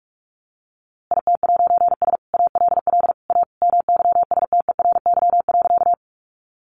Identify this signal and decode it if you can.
Morse code, heard as a keyed tone, about 44 wpm, 700 Hz. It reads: IT'S ALL A MYSTERY.
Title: A Sound Of Thunder 45wpm